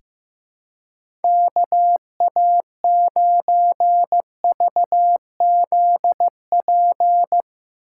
Morse code, heard as a keyed tone, about 15 wpm, 700 Hz. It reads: KA9VZP
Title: USA Call Signs - 6 Characters - 4x Character Spacing - 15wpm